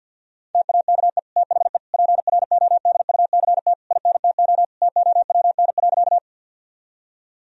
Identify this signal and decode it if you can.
Morse code, heard as a keyed tone, about 50 wpm, 700 Hz. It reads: TAKE THE PRODUCT INTO TOWN.
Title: Sentences from Top 1000 Words - Review Part 2 - 50wpm